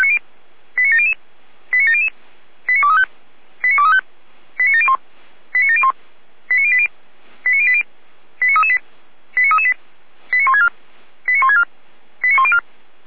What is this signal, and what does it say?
Five-tone selective call (ZVEI), heard as quick runs of five tones, tones of about 80 ms. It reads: 89700 89700 89235 89235 89701 89701 89080 89080 89338 89338 89164 89164 89104